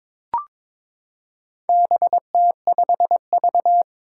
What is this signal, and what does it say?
Morse code, heard as a keyed tone, about 22 wpm, 700 Hz. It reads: BT5V
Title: Four Challenging Letter-Number Combinations 22wpm